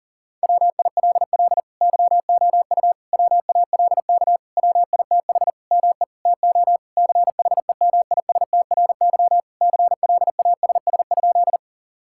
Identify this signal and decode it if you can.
Morse code, heard as a keyed tone, about 41 wpm, 700 Hz. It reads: WILL YOU WALK WITH ME TO CHEMISTRY CLASS?
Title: Sentences from Top 1000 Words - Review Part 6 - 40wpm